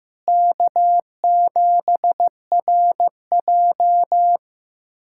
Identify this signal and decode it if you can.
Morse code, heard as a keyed tone, about 15 wpm, 700 Hz. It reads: K7RJ
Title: USA Callsign DE Callsign - 4 Characters - 15wpm